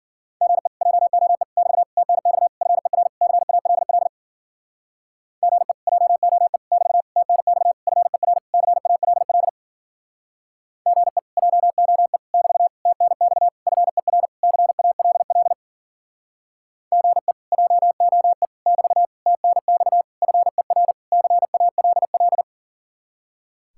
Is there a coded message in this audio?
Yes: Morse code from 0.4 to 22.4 s, heard as a keyed tone, about 52 wpm, 700 Hz. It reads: GE JOE <BT> TNX FER CALL GE JOE <BT> TNX FER CALL GE JOE <BT> TNX FER CALL GE JOE <BT> TNX FER CALL